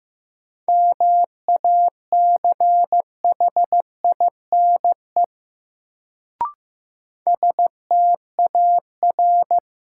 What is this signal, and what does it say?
Morse code, heard as a keyed tone, about 15 wpm, 700 Hz. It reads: MACHINE STAR
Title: Sentences from Top 1000 Words - Review Part 2 - New Word Drill - 15wpm